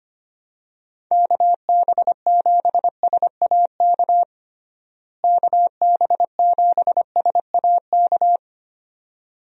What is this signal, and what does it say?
Morse code, heard as a keyed tone, about 25 wpm, 700 Hz. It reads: KB7SAK KB7SAK
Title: USA Call Signs - 6 Characters - Repeated 2x - 25wpm